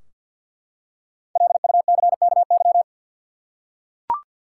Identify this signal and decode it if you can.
Morse code, heard as a keyed tone, about 49 wpm, 700 Hz. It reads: LUCKY
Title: Five Letter Words - 50wpm